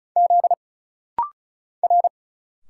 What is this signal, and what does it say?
Morse code, heard as a keyed tone, about 35 wpm, 700 Hz. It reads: Z R
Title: Single Letter-Number Alt - Rapid-Fire 35wpm